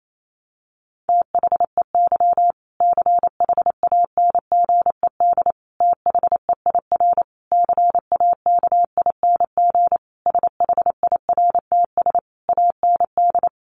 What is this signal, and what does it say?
Morse code, heard as a keyed tone, about 28 wpm, 700 Hz. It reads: THEY CHANGED THEIR CAKING SHIRTS AND
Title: A Sound Of Thunder 28wpm